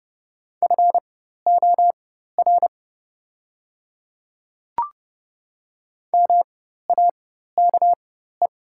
Morse code, heard as a keyed tone, about 30 wpm, spaced as for 14 wpm, 690 Hz. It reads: FOR MAKE
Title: 4x Character Spacing - Top 100 Words - 30wpm